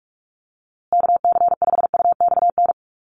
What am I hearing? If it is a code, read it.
Morse code, heard as a keyed tone, about 44 wpm, 700 Hz. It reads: KC5UXN
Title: USA Call Signs Repeated 2x 45wpm